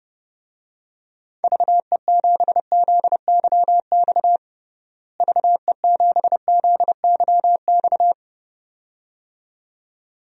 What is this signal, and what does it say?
Morse code, heard as a keyed tone, about 30 wpm, 700 Hz. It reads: VE7ZYX VE7ZYX
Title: DXCC-100 Call Signs - 6 Characters - Repeated 2x - 30wpm